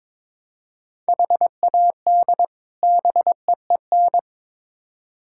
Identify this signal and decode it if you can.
Morse code, heard as a keyed tone, about 22 wpm, 700 Hz. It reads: HAD BEEN
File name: Build A Phrase - 2-Word Phrases - 22wpm